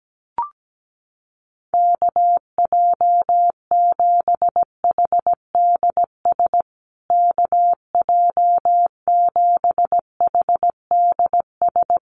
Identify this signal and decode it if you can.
Morse code, heard as a keyed tone, about 17 wpm, 700 Hz. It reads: KJ7HDS KJ7HDS
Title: USA Call Signs Repeated 3x 17wpm